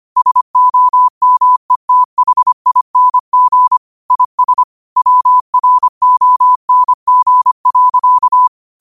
Morse code, heard as a keyed tone, about 25 wpm, 1000 Hz. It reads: SOMETHING IS WRONG.
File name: qotdfiles / 25wpm